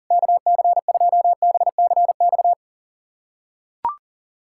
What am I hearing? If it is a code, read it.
Morse code, heard as a keyed tone, about 40 wpm, 700 Hz. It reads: KC2BCX
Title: USA Callsign DE Callsign - 6 Characters - 40wpm